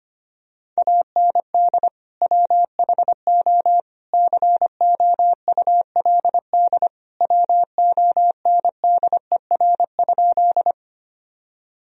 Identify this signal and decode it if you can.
Morse code, heard as a keyed tone, about 25 wpm, 700 Hz. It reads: AND WHO COULD WONDER?
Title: Sentences from Top 1000 Words - Review Part 3 - 25wpm